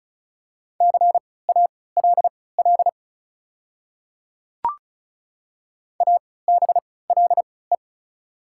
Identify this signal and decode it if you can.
Morse code, heard as a keyed tone, about 36 wpm, spaced as for 20 wpm, 690 Hz. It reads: CALL ABLE
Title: 3x Character Spacing - Top 500 Words - 35wpm